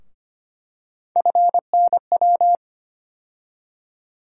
Morse code, heard as a keyed tone, about 25 wpm, 700 Hz. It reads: FNW
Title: USA Call Sign Postfixes - 3 Letters 25wpm